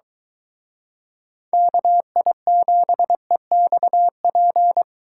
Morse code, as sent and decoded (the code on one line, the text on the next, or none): -.- .. --... -..- .--.
KI7EXP